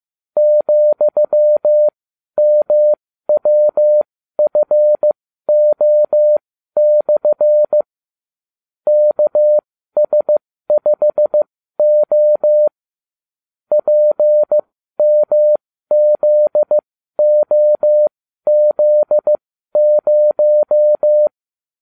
Morse code, heard as a keyed tone, about 15 wpm, spaced as for 12 wpm, 600 Hz. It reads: ,MWFO/ KS5O PMZOZ0
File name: cw / ubacw30 3